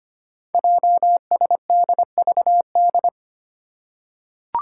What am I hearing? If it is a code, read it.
Morse code, heard as a keyed tone, about 26 wpm, 690 Hz. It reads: JSDVD
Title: Code Groups - 5 Letters - 25wpm